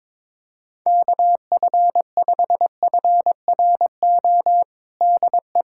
Morse code, heard as a keyed tone, about 22 wpm, 700 Hz. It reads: KF5FRO DE
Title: USA Callsign DE Callsign - 6 Characters - 22wpm